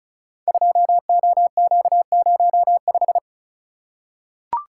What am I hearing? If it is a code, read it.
Morse code, heard as a keyed tone, about 36 wpm, 690 Hz. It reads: 2OQ05
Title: Code Groups - 5 Letter:Numbers - 35wpm